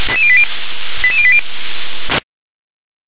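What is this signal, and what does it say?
Five-tone selective call (ZVEI), heard as quick runs of five tones, tones of about 70 ms. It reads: 88080
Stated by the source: ZVEI five-tone selective call 88646, received by SDR.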